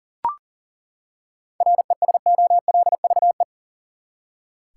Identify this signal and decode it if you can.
Morse code, heard as a keyed tone, about 40 wpm, 700 Hz. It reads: RESOLVE
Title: Seven Letter Words - 40wpm